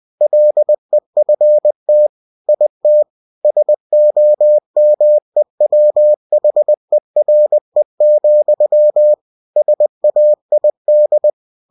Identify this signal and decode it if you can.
Morse code, heard as a keyed tone, about 20 wpm, 600 Hz. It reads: LEFT IT SOMEWHERE, SAID
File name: cw-books / winnie-the-pooh / winnie-20wpm-page-0027